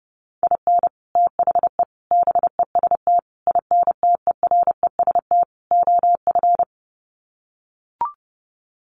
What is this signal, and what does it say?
Morse code, heard as a keyed tone, about 30 wpm, 700 Hz. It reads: IN THE BEST INTEREST OF
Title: Sets of 5 Words Encore 30wpm